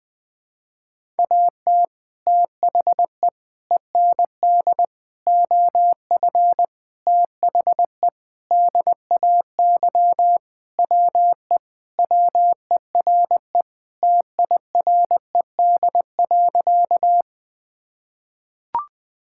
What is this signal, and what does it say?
Morse code, heard as a keyed tone, about 20 wpm, 700 Hz. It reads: AT THE END OF THE DAY WE WERE TIRED.
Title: Sentences with 5-Word Phrases - 20wpm